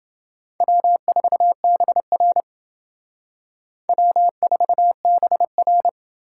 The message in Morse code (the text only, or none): W4BR W4BR